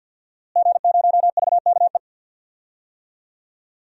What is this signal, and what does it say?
Morse code, heard as a keyed tone, about 52 wpm, 700 Hz. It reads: G0VKE
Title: DXCC-100 Call Signs - 5 Characters - Repeated 3x - 50wpm